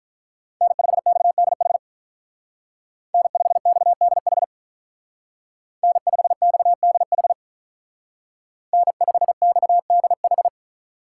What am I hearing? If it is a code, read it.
Morse code, heard as a keyed tone, about 43 wpm, 700 Hz. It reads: N5XDH N5XDH N5XDH N5XDH
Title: Speed Racing - USA 1x3 Call Signs 35wpm